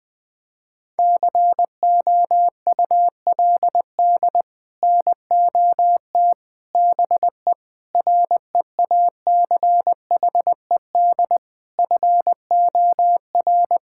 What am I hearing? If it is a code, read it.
Morse code, heard as a keyed tone, about 21 wpm, 700 Hz. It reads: COULD NOT BE REACHED FOR
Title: Sets of 6 Words - 20wpm